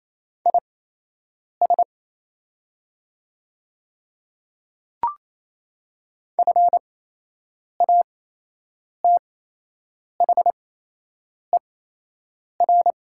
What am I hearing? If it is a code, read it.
Morse code, heard as a keyed tone, about 28 wpm, spaced as for 8 wpm, 700 Hz. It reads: IS FATHER
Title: 8x Character Spacing - Top 200 Words - 28wpm